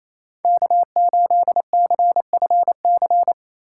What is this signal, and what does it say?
Morse code, heard as a keyed tone, about 28 wpm, 700 Hz. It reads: K8CFC
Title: USA Call Signs - 5 Characters - 4x Character Spacing - 28wpm